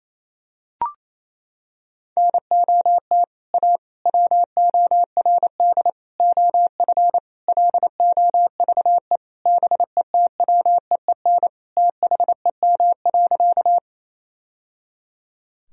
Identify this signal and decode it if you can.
Morse code, heard as a keyed tone, about 28 wpm, 700 Hz. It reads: NOT A WORD OF LOVE BETWEEN THEM.
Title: Sentences from Top 1000 Words - Review Part 1 - 28wpm